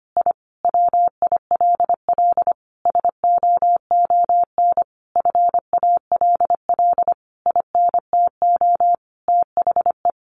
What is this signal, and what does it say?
Morse code, heard as a keyed tone, about 25 wpm, 700 Hz. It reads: I WILL SOON FALL INTO THE
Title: Sentences from Top 1300 Words 25wpm